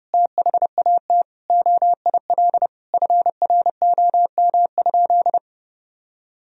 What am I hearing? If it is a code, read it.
Morse code, heard as a keyed tone, about 32 wpm, 700 Hz. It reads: THAT OIL FROM?
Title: Sentences from Top 1000 Words - Review Part 4 - 30wpm